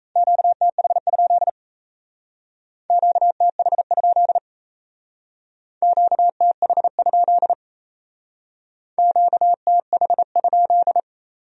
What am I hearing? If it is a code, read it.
Morse code, heard as a keyed tone, about 35 wpm, 700 Hz. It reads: QTH? QTH? QTH? QTH?
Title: Speed-Racing Q-Codes - 28wpm